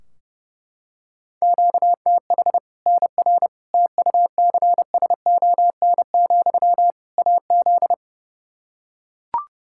Morse code, heard as a keyed tone, about 32 wpm, 700 Hz. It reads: QTH NR TUCSON, AZ
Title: K1USN SST QSO Practice 30wpm